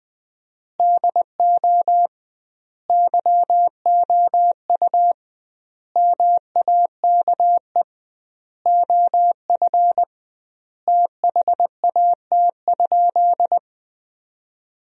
Morse code, heard as a keyed tone, about 20 wpm, 700 Hz. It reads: DO YOU MAKE OF THAT?
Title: 2x Word-Spacing - Sentences from Top 100 Words - 20wpm